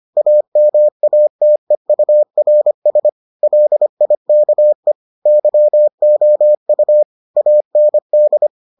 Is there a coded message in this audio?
Yes: Morse code from 0.2 to 8.5 s, heard as a keyed tone, about 25 wpm, 600 Hz. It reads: AMATEURS LIKE YOU AND